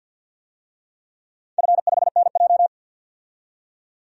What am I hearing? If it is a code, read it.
Morse code, heard as a keyed tone, about 53 wpm, 700 Hz. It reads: F5NJ
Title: DXCC-100 Call Signs - 4 Characters - 50wpm